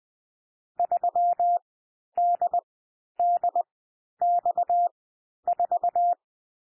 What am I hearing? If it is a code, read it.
Morse code, heard as a keyed tone, about 20 wpm, spaced as for 11 wpm, 700 Hz. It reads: 3DDX4
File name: cwmp3 / M1 SOUND MIXED 20 wpm